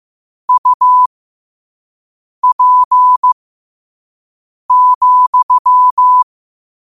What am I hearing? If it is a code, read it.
Morse code, heard as a keyed tone, about 15 wpm, spaced as for 5 wpm, 1000 Hz. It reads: UP,
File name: qotdfiles / 5wpm